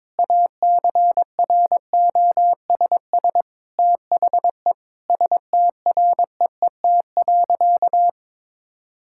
Morse code, heard as a keyed tone, about 22 wpm, 700 Hz. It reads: ACROSS THE STREET.
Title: Sentences from Top 1000 Words - Review Part 5 - 22wpm